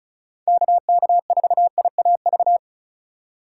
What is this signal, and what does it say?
Morse code, heard as a keyed tone, about 36 wpm, 700 Hz. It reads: KK4IAV
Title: USA Call Signs - 6 Characters - 2x Character Spacing - 35wpm